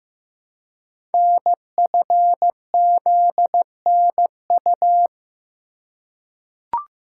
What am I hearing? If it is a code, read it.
Morse code, heard as a keyed tone, about 15 wpm, 700 Hz. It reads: NFZNU
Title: Code Groups - 5 Letters - 15wpm